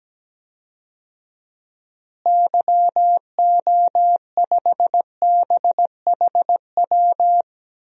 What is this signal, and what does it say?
Morse code, heard as a keyed tone, about 17 wpm, 700 Hz. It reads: YO5BHW
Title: DXCC-100 Call Signs - 6 Characters - Repeated 2x - 17wpm